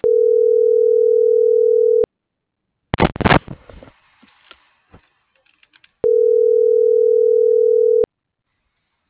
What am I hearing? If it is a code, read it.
Ringback tone (North America), 440 + 480 Hz.